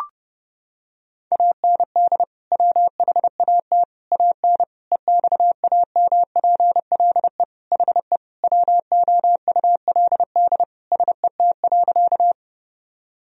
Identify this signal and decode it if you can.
Morse code, heard as a keyed tone, about 30 wpm, 700 Hz. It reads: AND WHAT AN EXAMPLE HE WOULD SET.
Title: Sentences from Top 1000 Words - Review Part 1 - 30wpm